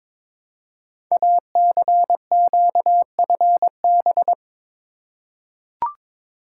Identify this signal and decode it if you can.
Morse code, heard as a keyed tone, about 22 wpm, 700 Hz. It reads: ACQFB
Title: Code Groups - 5 Letters - 22wpm